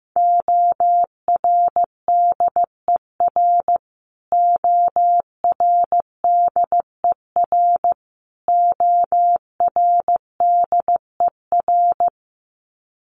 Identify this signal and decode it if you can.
Morse code, heard as a keyed tone, about 15 wpm, 700 Hz. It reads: ORDER ORDER ORDER
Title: Top 500 Words Repeated 15wpm